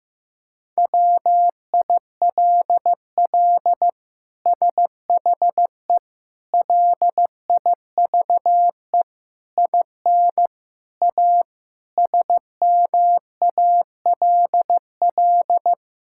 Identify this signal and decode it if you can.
Morse code, heard as a keyed tone, about 15 wpm, 700 Hz. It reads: WILL SHE LIVE IN A SMALL
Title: Sentences from Top 1000 Words - Review Part 6 - 15wpm